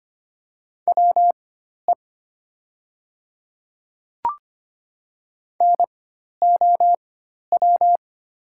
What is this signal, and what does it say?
Morse code, heard as a keyed tone, about 25 wpm, spaced as for 12 wpm, 700 Hz. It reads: WE NOW
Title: 4x Character Spacing - Top 100 Words - 25wpm